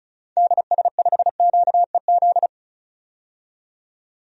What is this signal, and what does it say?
Morse code, heard as a keyed tone, about 36 wpm, 700 Hz. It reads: DS5QEZ
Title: DXCC-100 Call Signs - 6 Characters - 35wpm